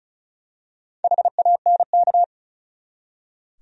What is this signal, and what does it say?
Morse code, heard as a keyed tone, about 36 wpm, 700 Hz. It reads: HANK
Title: USA Common Names - Max 4 Letters - 35wpm